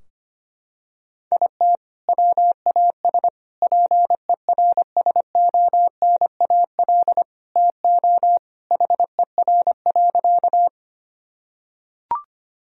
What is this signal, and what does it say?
Morse code, heard as a keyed tone, about 25 wpm, 700 Hz. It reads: IT WAS PERSONAL TO HER.
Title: Sentences from Top 1000 Words - Review Part 4 - 25wpm